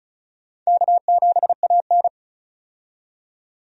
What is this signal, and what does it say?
Morse code, heard as a keyed tone, about 36 wpm, 690 Hz. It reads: K7AN
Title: USA Call Signs - 4 Characters - 35wpm